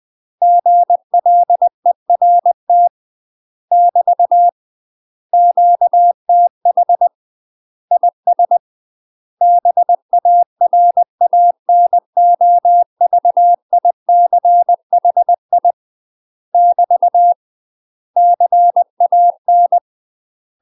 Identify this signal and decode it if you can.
Morse code, heard as a keyed tone, about 20 wpm, 700 Hz. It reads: GLERT <BT> QTH IS BARANOVICHI <BT> CAN
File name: UK MASTER QSOs TRAINING / 20WPM / UK MASTER QSOs 0052